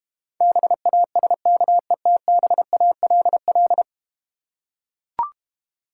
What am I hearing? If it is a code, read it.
Morse code, heard as a keyed tone, about 32 wpm, 690 Hz. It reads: BASKETBALL